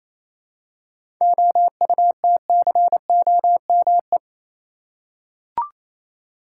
Morse code, heard as a keyed tone, about 28 wpm, 700 Hz. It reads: OUTCOME
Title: Seven Letter Words - 28wpm